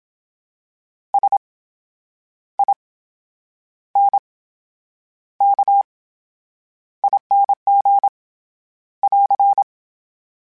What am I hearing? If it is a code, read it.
Morse code, heard as a keyed tone, about 27 wpm, spaced as for 9 wpm, 800 Hz. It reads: SINKING <AR>
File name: Special INGs / Special INGs-mp3s / VerySlowToFaster Ing @25